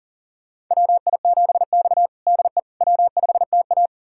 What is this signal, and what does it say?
Morse code, heard as a keyed tone, about 37 wpm, 700 Hz. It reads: WI7X DE W5TA
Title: USA Callsign DE Callsign - 4 Characters - 40wpm